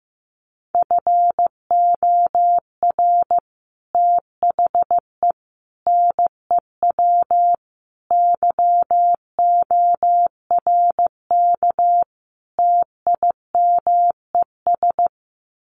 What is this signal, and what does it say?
Morse code, heard as a keyed tone, about 14 wpm, 700 Hz. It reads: FOR THE NEW YORK TIMES
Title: Sets of 5 Words 15wpm